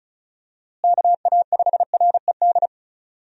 Morse code, heard as a keyed tone, about 36 wpm, 700 Hz. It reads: KA5RED
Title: USA Call Signs - 6 Characters - Repeated 3x - 35wpm